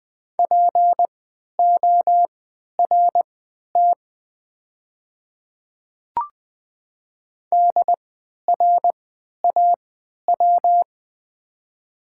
Morse code, heard as a keyed tone, about 20 wpm, spaced as for 11 wpm, 700 Hz. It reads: PORT DRAW
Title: 3x Character Spacing - Top 300 Words - 20wpm